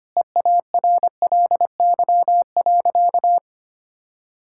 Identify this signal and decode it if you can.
Morse code, heard as a keyed tone, about 25 wpm, 700 Hz. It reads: EARLY.
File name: practice / 25wpm / Sentences with 3-Word Phrases - 25wpm